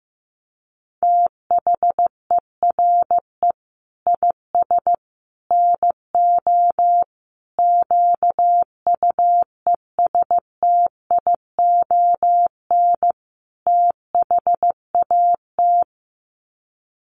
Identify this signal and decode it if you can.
Morse code, heard as a keyed tone, about 15 wpm, 700 Hz. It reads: THERE IS NO QUESTION THAT
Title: Sets of 5 Words Encore 15wpm